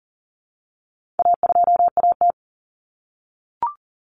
Morse code, heard as a keyed tone, about 37 wpm, 700 Hz. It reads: A2AT
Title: Common Cut Numbers - 4-Digit 40wpm